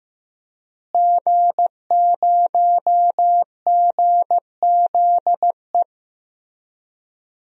Morse code, heard as a keyed tone, about 16 wpm, 700 Hz. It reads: G0GZE
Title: DXCC-100 Call Signs - 5 Characters - Repeated 3x - 15wpm